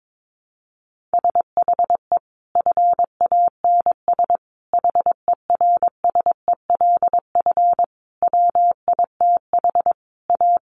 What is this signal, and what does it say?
Morse code, heard as a keyed tone, about 22 wpm, 700 Hz. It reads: SHE FANS HERSELF WITH A
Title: Additional Easy Phrases 22wpm